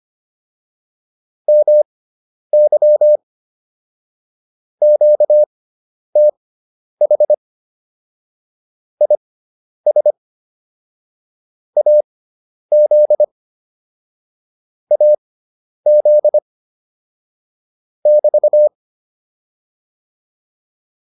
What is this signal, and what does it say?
Morse code, heard as a keyed tone, about 25 wpm, spaced as for 10 wpm, 600 Hz. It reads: MY QTH IS AZ AZ <BT>